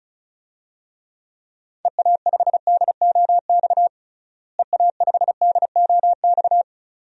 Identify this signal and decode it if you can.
Morse code, heard as a keyed tone, about 36 wpm, 700 Hz. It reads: EA5DOX EA5DOX
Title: DXCC-100 Call Signs - 6 Characters - Repeated 2x - 35wpm